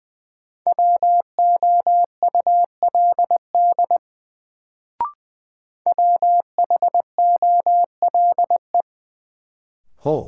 Morse code, heard as a keyed tone, about 20 wpm, 700 Hz. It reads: WOULD WHOLE
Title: Sets of 2 Words - New Word Drill - 20wpm